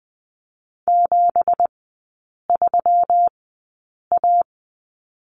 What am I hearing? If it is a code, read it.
Morse code, heard as a keyed tone, about 19 wpm, spaced as for 8 wpm, 690 Hz. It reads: 73A
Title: 166 Lesson 27e - Number 7 - Three Characters 20wpm